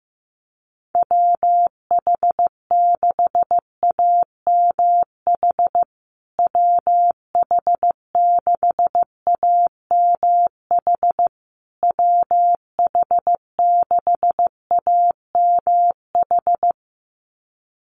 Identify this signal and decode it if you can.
Morse code, heard as a keyed tone, about 15 wpm, 700 Hz. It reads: WH6AMH WH6AMH WH6AMH
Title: USA Call Signs Repeated 3x 15wpm